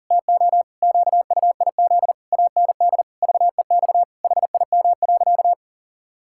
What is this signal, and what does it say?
Morse code, heard as a keyed tone, about 40 wpm, 700 Hz. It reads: TO QUIZ AND VEX HIM.
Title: Pangram Challenge v2 - 40wpm